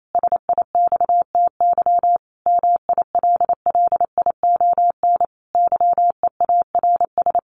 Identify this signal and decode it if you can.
Morse code, heard as a keyed tone, about 28 wpm, 700 Hz. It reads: SIXTY MILLION YEARS
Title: A Sound Of Thunder 28wpm